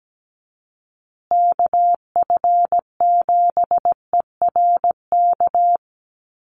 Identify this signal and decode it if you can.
Morse code, heard as a keyed tone, about 17 wpm, 700 Hz. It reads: KF7ERK
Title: USA Call Signs Repeated 3x 17wpm